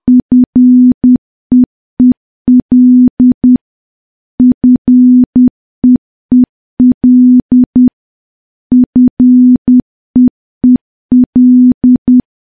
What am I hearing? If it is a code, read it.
Morse code, heard as a keyed tone, about 10 wpm, 250 Hz. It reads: FEEL FEEL FEEL
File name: morse / morse-training-10WPM